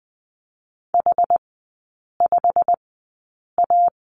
Morse code, heard as a keyed tone, about 20 wpm, spaced as for 8 wpm, 700 Hz. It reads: H5A
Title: 191 Lesson 31e - Letter v - Three Characters 20wpm